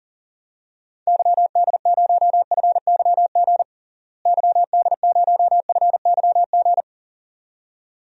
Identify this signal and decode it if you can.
Morse code, heard as a keyed tone, about 40 wpm, 700 Hz. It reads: YD0FYG YD0FYG
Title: DXCC-100 Call Signs - 6 Characters - Repeated 2x - 40wpm